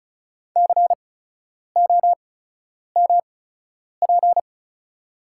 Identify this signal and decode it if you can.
Morse code, heard as a keyed tone, about 34 wpm, spaced as for 10 wpm, 690 Hz. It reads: COMP
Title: 8x Character Spacing - Top 500 Words - 35wpm